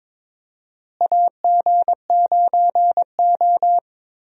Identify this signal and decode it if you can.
Morse code, heard as a keyed tone, about 21 wpm, 700 Hz. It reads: AG9O
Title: USA Call Signs - 4 Characters - Repeated 2x - 22wpm